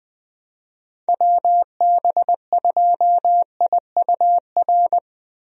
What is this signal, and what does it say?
Morse code, heard as a keyed tone, about 20 wpm, 700 Hz. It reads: WB2IUR
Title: USA Call Signs - 6 Characters - 20wpm